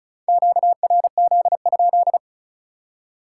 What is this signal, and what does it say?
Morse code, heard as a keyed tone, about 36 wpm, 700 Hz. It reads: QRZ?
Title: Top 180 QSO Elements 35wpm